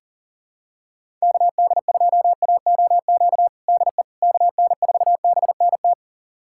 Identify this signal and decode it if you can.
Morse code, heard as a keyed tone, about 37 wpm, 700 Hz. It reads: KD2AOQ DE KN4BNT
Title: USA Callsign DE Callsign - 6 Characters - 40wpm